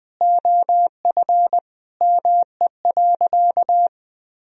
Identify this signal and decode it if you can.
Morse code, heard as a keyed tone, about 20 wpm, 700 Hz. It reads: OF ME.
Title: Sentences from Top 1000 Words - Review Part 2 - 20wpm